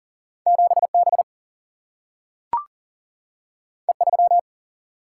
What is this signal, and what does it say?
Morse code, heard as a keyed tone, about 39 wpm, 700 Hz. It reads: 7 B E 3
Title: Single-Letter Single-Number 40wpm